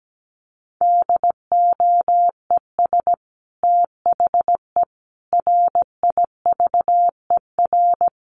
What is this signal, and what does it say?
Morse code, heard as a keyed tone, about 17 wpm, 700 Hz. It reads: DOES THE RIVER
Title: Sentences from Top 1200 Words 17wpm